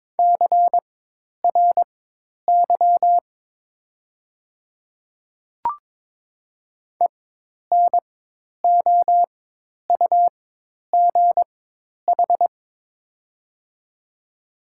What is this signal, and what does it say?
Morse code, heard as a keyed tone, about 22 wpm, spaced as for 10 wpm, 700 Hz. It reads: CRY ENOUGH
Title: 4x Character Spacing - Top 500 Words - 22wpm